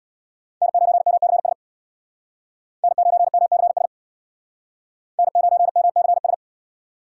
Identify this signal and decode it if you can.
Morse code, heard as a keyed tone, about 62 wpm, 700 Hz. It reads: N9MCS N9MCS N9MCS
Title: Speed Racing - USA 1x3 Call Signs 50wpm